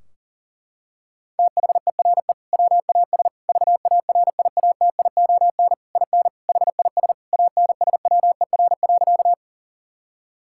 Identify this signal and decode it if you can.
Morse code, heard as a keyed tone, about 40 wpm, 700 Hz. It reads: THERE WAS VARIATION IN HIS ANSWER.
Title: Sentences from Top 1600 Words 40wpm